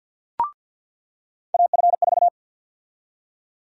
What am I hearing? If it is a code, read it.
Morse code, heard as a keyed tone, about 53 wpm, 700 Hz. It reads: AF4
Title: USA Call Sign Prefixes - 50wpm